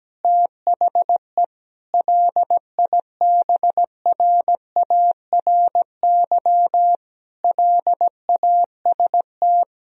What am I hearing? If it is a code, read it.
Morse code, heard as a keyed tone, about 17 wpm, 700 Hz. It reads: THE LIBRARY LAST
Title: Sentences from Top 1000 Words - Review Part 4 - 17wpm